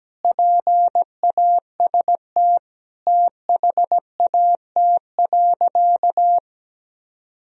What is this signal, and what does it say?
Morse code, heard as a keyed tone, about 17 wpm, 700 Hz. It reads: PAST THAT.
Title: Sentences from Top 1000 Words - Review Part 3 - 17wpm